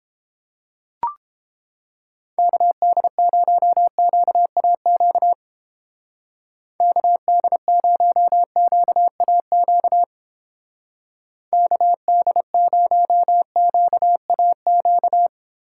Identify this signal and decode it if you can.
Morse code, heard as a keyed tone, about 29 wpm, 700 Hz. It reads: KD0QAQ KD0QAQ KD0QAQ